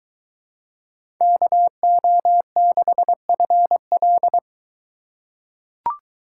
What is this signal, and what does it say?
Morse code, heard as a keyed tone, about 21 wpm, 700 Hz. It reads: KO6FL